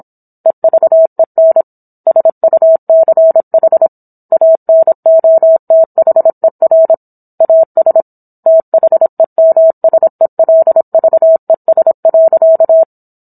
Morse code, heard as a keyed tone, about 27 wpm, 650 Hz. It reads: EVEN SUCH ANOTHER AS THEMSELVES.